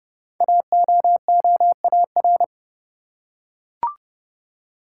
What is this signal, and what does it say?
Morse code, heard as a keyed tone, about 30 wpm, 700 Hz. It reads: AOOAR
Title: Code Groups - 5 Letter:Numbers - 30wpm